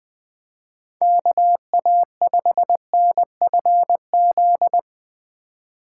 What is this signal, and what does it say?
Morse code, heard as a keyed tone, about 23 wpm, 700 Hz. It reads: KA5NFZ
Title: Speed Racing - USA 2x3 Call Signs 20wpm